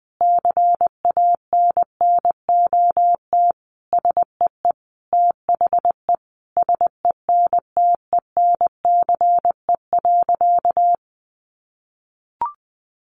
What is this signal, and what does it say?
Morse code, heard as a keyed tone, about 20 wpm, 700 Hz. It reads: CANNOT SEE THE SENTENCE.
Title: Additional Easy Phrases 20wpm